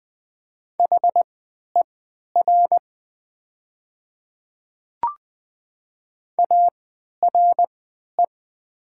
Morse code, heard as a keyed tone, about 20 wpm, spaced as for 11 wpm, 700 Hz. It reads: HER ARE
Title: 3x Character Spacing - Top 100 Words - 20wpm